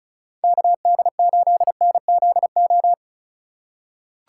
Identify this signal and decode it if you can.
Morse code, heard as a keyed tone, about 34 wpm, 700 Hz. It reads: KD8NZO
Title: USA Call Signs - 6 Characters - 4x Character Spacing - 35wpm